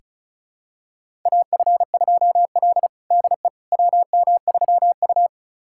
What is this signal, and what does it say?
Morse code, heard as a keyed tone, about 36 wpm, 700 Hz. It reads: AF2L DE WM3U
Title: USA Callsign DE Callsign - 4 Characters - 35wpm